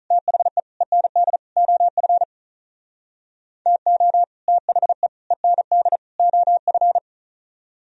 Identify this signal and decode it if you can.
Morse code, heard as a keyed tone, about 45 wpm, 700 Hz. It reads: THE END OF TO THE END OF